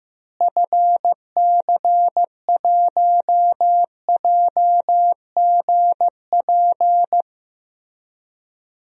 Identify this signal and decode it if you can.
Morse code, heard as a keyed tone, about 15 wpm, 700 Hz. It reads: FC1JGP
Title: DXCC-100 Call Signs - 6 Characters - Repeated 3x - 15wpm